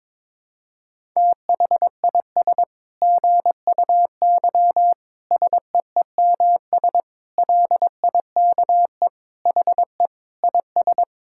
Morse code, heard as a keyed tone, about 22 wpm, 700 Hz. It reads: THIS GUY SEEMS LIKE HE IS
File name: Sentences with 9-Word Phrases - 22wpm